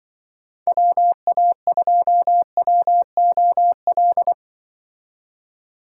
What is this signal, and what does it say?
Morse code, heard as a keyed tone, about 24 wpm, 700 Hz. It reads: WA2WOL
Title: Speed Racing - USA 2x3 Call Signs 20wpm